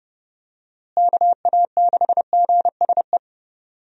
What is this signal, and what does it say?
Morse code, heard as a keyed tone, about 30 wpm, 700 Hz. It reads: KA6GSE
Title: USA Call Signs - 6 Characters - Repeated 3x - 30wpm